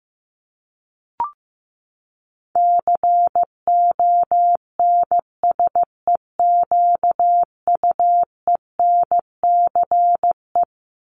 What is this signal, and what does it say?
Morse code, heard as a keyed tone, about 15 wpm, 700 Hz. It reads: CONSEQUENCE
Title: LY Suffix - Top 100 Words with Base Words 15wpm